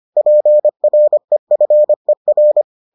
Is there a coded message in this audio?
Yes: Morse code from 0.2 to 2.6 s, heard as a keyed tone, about 25 wpm, 600 Hz. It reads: PREFER